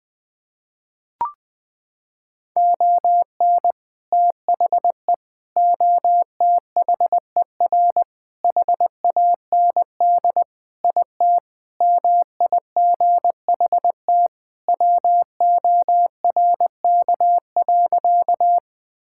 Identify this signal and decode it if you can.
Morse code, heard as a keyed tone, about 21 wpm, 700 Hz. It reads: ON THE OTHER HAND IT MIGHT WORK.
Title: Sentences with 3-Word Phrases - 20wpm